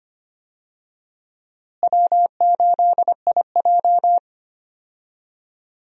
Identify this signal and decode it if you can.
Morse code, heard as a keyed tone, about 25 wpm, 700 Hz. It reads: W8IJ
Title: USA Call Signs - 4 Characters - Repeated 2x - 25wpm